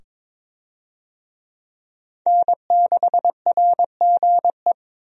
Morse code, heard as a keyed tone, about 22 wpm, 700 Hz. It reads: N6RGE
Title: USA Call Signs - 5 Characters - 22wpm